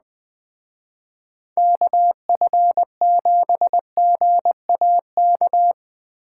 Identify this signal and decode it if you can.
Morse code, heard as a keyed tone, about 20 wpm, 700 Hz. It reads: KF7GAK
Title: Speed Racing - USA 2x3 Call Signs 17wpm